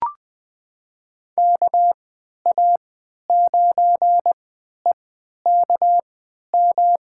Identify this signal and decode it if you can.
Morse code, heard as a keyed tone, about 20 wpm, spaced as for 11 wpm, 700 Hz. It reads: KA9EKM
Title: USA Call Signs - 3x Character Spacing - 20wpm